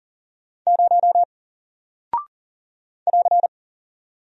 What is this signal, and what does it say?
Morse code, heard as a keyed tone, about 40 wpm, 700 Hz. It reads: 0 <AR>
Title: Single Character (Letter:Number:Punctuation:Prosign) 40wpm